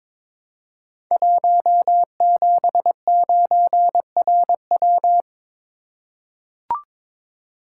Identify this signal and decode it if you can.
Morse code, heard as a keyed tone, about 22 wpm, 700 Hz. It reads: 179RW
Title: Code Groups - 5 Letter:Numbers - 22wpm